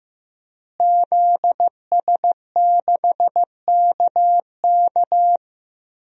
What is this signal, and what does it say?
Morse code, heard as a keyed tone, about 15 wpm, 700 Hz. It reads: ZS6KK
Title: DXCC-100 Call Signs - 5 Characters - Repeated 3x - 15wpm